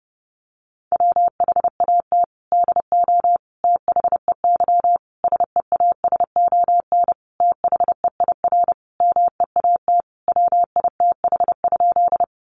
Morse code, heard as a keyed tone, about 30 wpm, 700 Hz. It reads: WHAT DO THEY SEASON THEIR MEAT WITH?
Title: Sentences from Top 1400 Words 30wpm